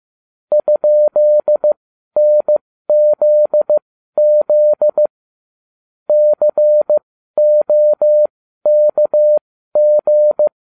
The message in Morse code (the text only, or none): ?NZZ COKG